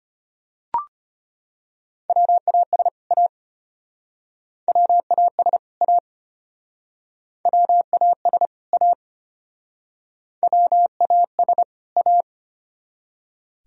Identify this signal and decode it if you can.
Morse code, heard as a keyed tone, about 34 wpm, 700 Hz. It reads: WAS A WAS A WAS A WAS A